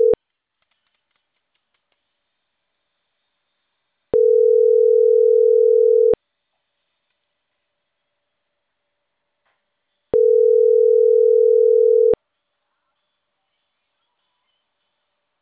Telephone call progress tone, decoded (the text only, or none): ringback tone